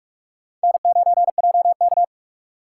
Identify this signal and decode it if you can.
Morse code, heard as a keyed tone, about 46 wpm, 700 Hz. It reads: N9JK
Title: USA Call Signs - 4 Characters - 4x Character Spacing - 45wpm